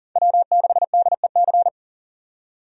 Morse code, heard as a keyed tone, about 40 wpm, 700 Hz. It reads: W6DEC
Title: USA Call Signs - 5 Characters - 40wpm